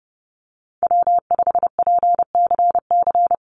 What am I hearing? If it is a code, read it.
Morse code, heard as a keyed tone, about 30 wpm, 700 Hz. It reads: W5PCC
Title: USA Call Signs 30wpm